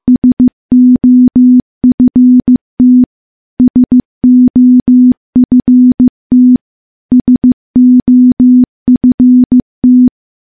Morse code, heard as a keyed tone, about 15 wpm, 250 Hz. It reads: SOFT SOFT SOFT
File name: morse-training-15WPM